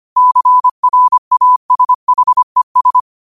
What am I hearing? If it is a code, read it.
Morse code, heard as a keyed tone, about 25 wpm, 1000 Hz. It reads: CRASHES
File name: qotdfiles / 25wpm